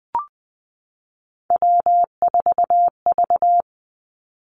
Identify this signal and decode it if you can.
Morse code, heard as a keyed tone, about 20 wpm, 700 Hz. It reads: W4V
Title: Three Challenging Letter-Number Combinations 20wpm